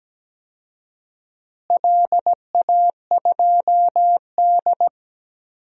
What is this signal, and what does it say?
Morse code, heard as a keyed tone, about 17 wpm, 700 Hz. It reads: LA2D
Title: DXCC-100 Call Signs - 4 Characters - Repeated 3x - 17wpm